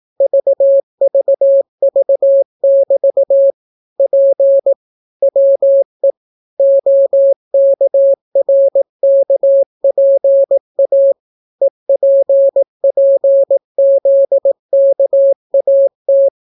Morse code, heard as a keyed tone, about 18 wpm, 550 Hz. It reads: VVV<BT> P WE OKRKPA EPPZKAT